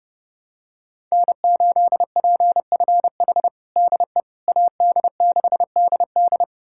Morse code, heard as a keyed tone, about 32 wpm, 700 Hz. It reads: N8PFH DE AD6DD
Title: USA Callsign DE Callsign - 5 Characters - 30wpm